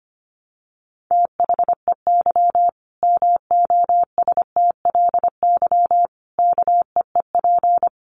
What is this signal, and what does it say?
Morse code, heard as a keyed tone, about 25 wpm, 700 Hz. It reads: THEY MOSTLY KEEP
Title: Sentences from Top 1300 Words 25wpm